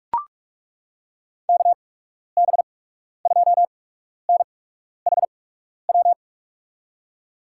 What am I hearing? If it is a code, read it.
Morse code, heard as a keyed tone, about 42 wpm, spaced as for 12 wpm, 700 Hz. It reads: KB2NHW
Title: USA Call Signs - 8x Character Spacing - 45wpm